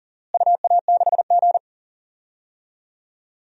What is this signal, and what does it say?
Morse code, heard as a keyed tone, about 40 wpm, 700 Hz. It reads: UA6G
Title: DXCC-100 Call Signs - 4 Characters - Repeated 3x - 40wpm